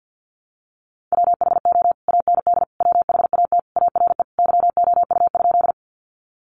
Morse code, heard as a keyed tone, about 50 wpm, 700 Hz. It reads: WHO AND WHAT ARE YOU?
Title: Sentences from Top 100 Words 50wpm